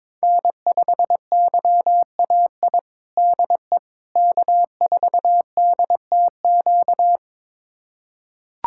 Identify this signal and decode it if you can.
Morse code, heard as a keyed tone, about 22 wpm, 700 Hz. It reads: N5YAI DE K4DTQ
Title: USA Callsign DE Callsign 22wpm